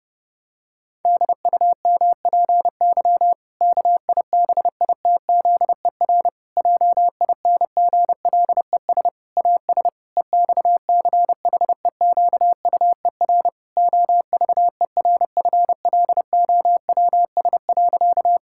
Morse code, heard as a keyed tone, about 30 wpm, 700 Hz. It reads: DUMPY KIBITZER JINGLES AS EXCHEQUER OVERFLOWS.
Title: Pangram Challenge v2 - 30wpm